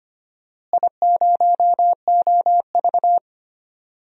Morse code, heard as a keyed tone, about 25 wpm, 700 Hz. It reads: I0OV
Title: DXCC-100 Call Signs - 4 Characters - Repeated 2x - 25wpm